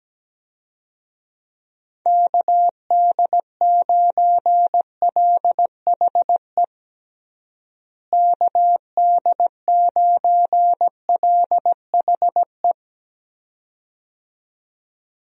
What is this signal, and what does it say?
Morse code, heard as a keyed tone, about 17 wpm, 700 Hz. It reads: KD9LHE KD9LHE
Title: USA Call Signs - 6 Characters - Repeated 2x - 17wpm